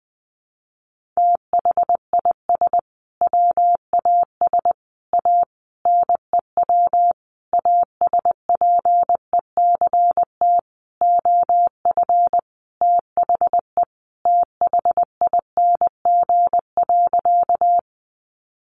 Morse code, heard as a keyed tone, about 20 wpm, 700 Hz. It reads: THIS WAS A NEW ASPECT OF THE THING.
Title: Sentences from Top 1300 Words 20wpm